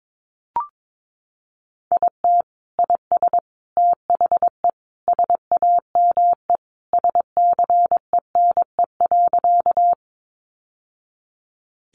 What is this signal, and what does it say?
Morse code, heard as a keyed tone, about 22 wpm, 700 Hz. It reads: IT IS THE SAME SCENE.
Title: Sentences from Top 1100 Words 22wpm